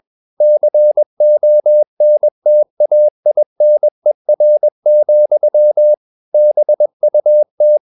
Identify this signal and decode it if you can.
Morse code, heard as a keyed tone, about 21 wpm, 600 Hz. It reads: CONTAINER, BUT
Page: CW med Gnister